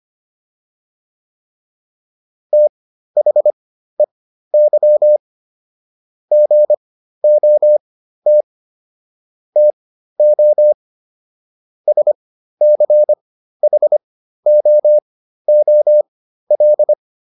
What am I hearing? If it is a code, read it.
Morse code, heard as a keyed tone, about 25 wpm, spaced as for 13 wpm, 600 Hz. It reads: THEY GOT TO SCHOOL